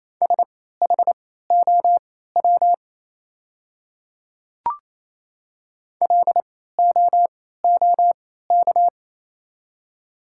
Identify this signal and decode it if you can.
Morse code, heard as a keyed tone, about 28 wpm, spaced as for 16 wpm, 700 Hz. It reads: SHOW LOOK
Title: 3x Character Spacing - Top 300 Words - 28wpm